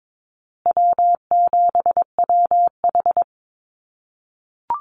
Morse code, heard as a keyed tone, about 22 wpm, 700 Hz. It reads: W7WH
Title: USA Call Signs Repeated 2x 22wpm